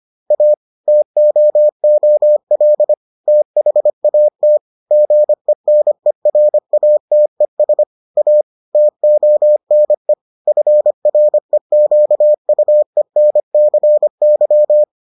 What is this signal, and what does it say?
Morse code, heard as a keyed tone, about 25 wpm, 600 Hz. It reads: A TOOL THAT GENERATES A TONE FREQUENCY